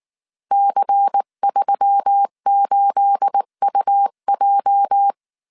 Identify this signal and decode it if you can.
Morse code, heard as a keyed tone, about 19 wpm, 800 Hz. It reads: C38UJ
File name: CW / DET / 1 9 1500C S B 18wpm 2551